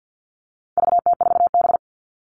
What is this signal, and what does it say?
Morse code, heard as a keyed tone, about 50 wpm, 700 Hz. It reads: VT4B
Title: All Cut Numbers - 4-Digit 50wpm